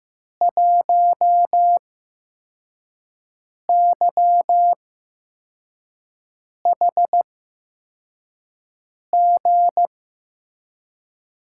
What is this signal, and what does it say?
Morse code, heard as a keyed tone, about 15 wpm, spaced as for 4 wpm, 700 Hz. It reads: 1YHG
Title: USA Call Signs - 5 Characters - 8x Character Spacing - 15wpm